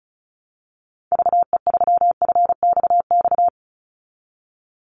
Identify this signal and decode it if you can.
Morse code, heard as a keyed tone, about 34 wpm, 700 Hz. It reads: VE3FXX
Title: Call Signs Repeated 3x - 35wpm